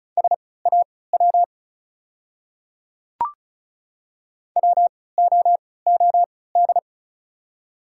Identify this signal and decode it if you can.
Morse code, heard as a keyed tone, about 36 wpm, spaced as for 20 wpm, 690 Hz. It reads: SAW WOOD
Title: 3x Character Spacing - Top 300 Words - 35wpm